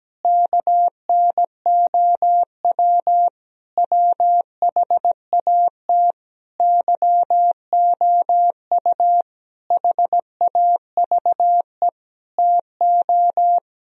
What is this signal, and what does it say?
Morse code, heard as a keyed tone, about 17 wpm, 700 Hz. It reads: KNOW WHAT YOU HAVE TO
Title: Sentences from Top 1000 Words - Review Part 3 - 17wpm